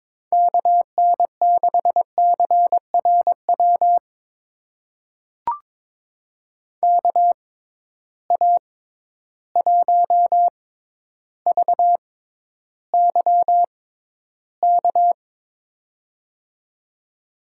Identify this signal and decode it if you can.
Morse code, heard as a keyed tone, about 22 wpm, spaced as for 11 wpm, 700 Hz. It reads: KN6CRW KA1VYK
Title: USA Call Signs - 6 Characters - 6x Character Spacing - 22wpm